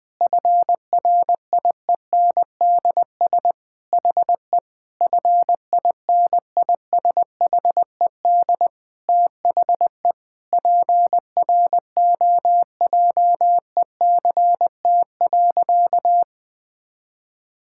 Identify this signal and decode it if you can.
Morse code, heard as a keyed tone, about 19 wpm, 700 Hz. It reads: FRIENDS HE FINISHED THE PROJECT.
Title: Sentences with 4-Word Phrases - Spoken Beginning - 20wpm